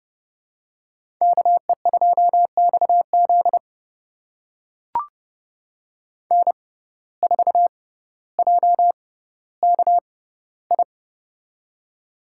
Morse code, heard as a keyed tone, about 30 wpm, spaced as for 15 wpm, 700 Hz. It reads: KE2XZ N4JKI
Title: USA Call Signs - 5 Characters - 6x Character Spacing - 30wpm